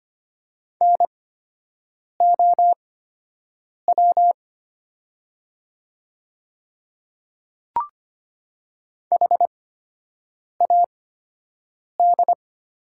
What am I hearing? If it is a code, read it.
Morse code, heard as a keyed tone, about 25 wpm, spaced as for 7 wpm, 700 Hz. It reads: NOW HAD
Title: 8x Character Spacing - Top 100 Words - 25wpm